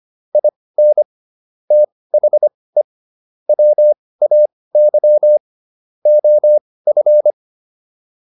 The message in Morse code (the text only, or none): IN THE WAY OF